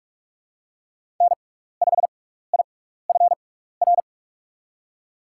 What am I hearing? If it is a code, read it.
Morse code, heard as a keyed tone, about 42 wpm, spaced as for 15 wpm, 700 Hz. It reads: N5IFR
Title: USA Call Signs - 5 Characters - 6x Character Spacing - 45wpm